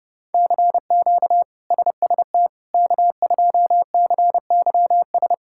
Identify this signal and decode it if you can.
Morse code, heard as a keyed tone, about 30 wpm, 700 Hz. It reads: CQ SST K2CYS
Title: K1USN SST QSO Practice - 30wpm